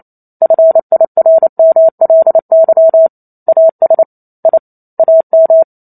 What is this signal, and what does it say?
Morse code, heard as a keyed tone, about 28 wpm, 650 Hz. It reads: FIRMLY AS I AM
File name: Chapter 2 - 28 WPM